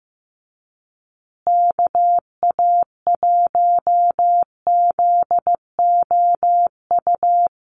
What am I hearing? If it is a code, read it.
Morse code, heard as a keyed tone, about 15 wpm, 700 Hz. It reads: KA1ZOU
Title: USA Call Signs Repeated 2x 15wpm